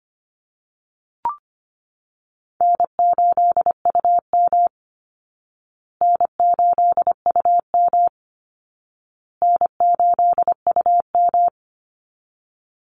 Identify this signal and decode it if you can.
Morse code, heard as a keyed tone, about 25 wpm, 700 Hz. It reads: N8UM N8UM N8UM
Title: Call Signs Repeated 2x - 25wpm